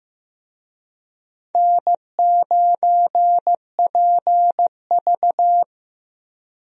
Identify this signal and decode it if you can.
Morse code, heard as a keyed tone, about 16 wpm, 700 Hz. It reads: N9PV
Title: USA Call Signs - 4 Characters - 15wpm